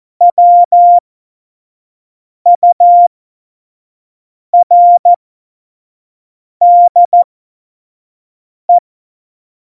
Morse code, heard as a keyed tone, about 14 wpm, spaced as for 5 wpm, 700 Hz. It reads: WURDE
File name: Morsekurs / Witze / Witz 29. September 2020, 25 F70